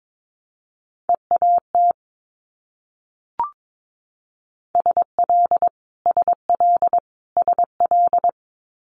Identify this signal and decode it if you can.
Morse code, heard as a keyed tone, about 22 wpm, 700 Hz. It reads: EAT SL SL SL
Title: Multi-letter Phonemes Repeated 3x 22wpm